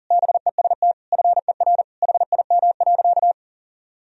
Morse code, heard as a keyed tone, about 40 wpm, 700 Hz. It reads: BEST FER HIM.
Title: Sentences from Top 500 Words and Abbreviations 40wpm